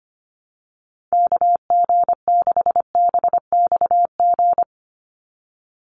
Morse code, heard as a keyed tone, about 25 wpm, 700 Hz. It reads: KG6BXG
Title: USA Call Signs 25wpm